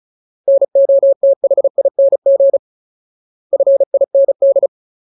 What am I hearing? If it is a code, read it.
Morse code, heard as a keyed tone, about 35 wpm, 550 Hz. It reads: NOTHING FIND
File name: Ultimate Word List / 35WPM Ultimate